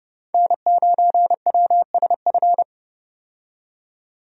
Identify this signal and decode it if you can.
Morse code, heard as a keyed tone, about 29 wpm, 700 Hz. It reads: N9WSF
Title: USA Call Signs - 5 Characters - 30wpm